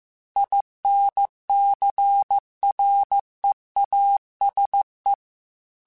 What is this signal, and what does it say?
Morse code, heard as a keyed tone, about 15 wpm, 800 Hz. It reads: INCREASE